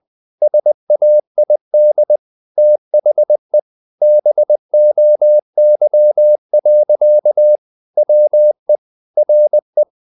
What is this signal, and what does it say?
Morse code, heard as a keyed tone, about 20 wpm, 600 Hz. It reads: SAID THE BOY. WE RE
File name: the old man and the sea / the old man and the sea part 1 20wpm